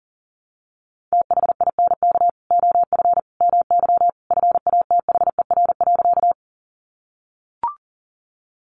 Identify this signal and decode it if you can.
Morse code, heard as a keyed tone, about 40 wpm, 700 Hz. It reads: THINK OF MY FATHER.
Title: Sentences from Top 300 Words 40wpm